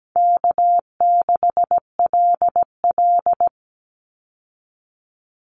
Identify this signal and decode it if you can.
Morse code, heard as a keyed tone, about 17 wpm, 700 Hz. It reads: K6LL
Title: Call Signs Repeated 3x - 17wpm